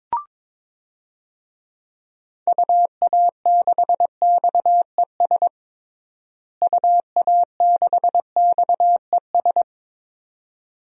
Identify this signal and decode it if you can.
Morse code, heard as a keyed tone, about 22 wpm, 700 Hz. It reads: UA6XES UA6XES
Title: DXCC-100 Call Signs - 6 Characters - Repeated 2x - 22wpm